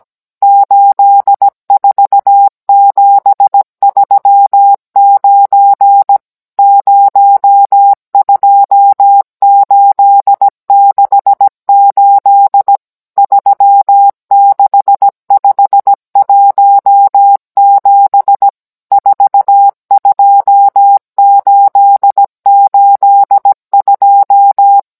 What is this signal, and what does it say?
Morse code, heard as a keyed tone, about 17 wpm, 800 Hz. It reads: 84739 02868 36517 42882